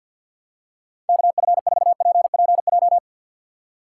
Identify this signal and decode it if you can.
Morse code, heard as a keyed tone, about 53 wpm, 700 Hz. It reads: KF4PPJ
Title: USA Call Signs - 6 Characters - 3x Character Spacing - 50wpm